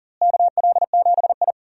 Morse code, heard as a keyed tone, about 40 wpm, 700 Hz. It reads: KL7I
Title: DXCC-100 Call Signs - 4 Characters - Repeated 2x - 40wpm